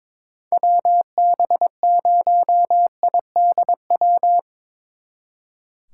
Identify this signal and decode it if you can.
Morse code, heard as a keyed tone, about 25 wpm, 700 Hz. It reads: WB0IDW
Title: Speed Racing - USA 2x3 Call Signs 22wpm